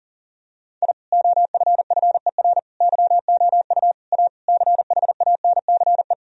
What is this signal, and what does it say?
Morse code, heard as a keyed tone, about 40 wpm, 700 Hz. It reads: I OFFER YOU A CHANCE
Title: Sentences from Top 1000 Words - Review Part 4 - 40wpm